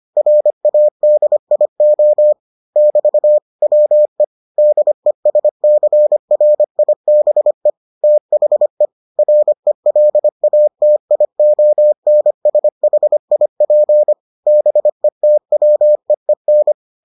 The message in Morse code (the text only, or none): RADIO <BT> WE DESCRIBE THE RELATIONSHIP BETWEEN